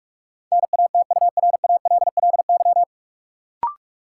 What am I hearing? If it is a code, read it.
Morse code, heard as a keyed tone, about 46 wpm, 700 Hz. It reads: NATURALLY
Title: Sentences from Top 1300 Words - New Word Drill Alt - 45wpm